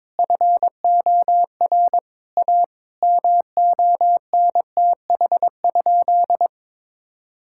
Morse code, heard as a keyed tone, about 22 wpm, 700 Hz. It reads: FOR A MONTH?
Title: Sentences from Top 1000 Words - Review Part 5 - 22wpm